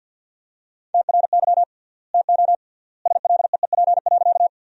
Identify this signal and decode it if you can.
Morse code, heard as a keyed tone, about 50 wpm, 700 Hz. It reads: TRY TO SLEEP.
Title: 3x Word-Spacing - Sentences from Top 500 Words - 50wpm